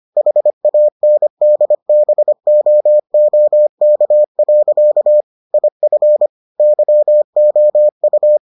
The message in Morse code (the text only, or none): HANDBOOK. IF YOU